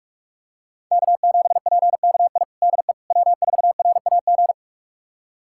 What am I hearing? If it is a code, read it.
Morse code, heard as a keyed tone, about 42 wpm, 700 Hz. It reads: K7PKI DE W4RAG
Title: USA Callsign DE Callsign - 5 Characters - 45wpm